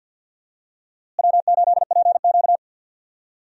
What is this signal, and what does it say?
Morse code, heard as a keyed tone, about 52 wpm, 700 Hz. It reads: W8PQ
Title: USA Call Signs - 4 Characters - Repeated 2x - 50wpm